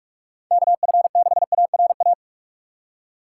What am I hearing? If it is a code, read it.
Morse code, heard as a keyed tone, about 46 wpm, 700 Hz. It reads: KF6ARA
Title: USA Call Signs - 6 Characters - 2x Character Spacing - 45wpm